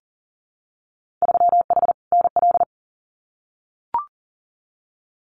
Morse code, heard as a keyed tone, about 40 wpm, 700 Hz. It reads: 3H NL
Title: Winter Field Day Exchanges Encore 40wpm